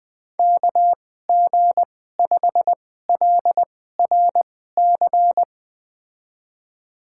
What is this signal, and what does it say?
Morse code, heard as a keyed tone, about 21 wpm, spaced as for 15 wpm, 700 Hz. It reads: KG5LRC
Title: USA Call Signs - 6 Characters - 2x Character Spacing - 20wpm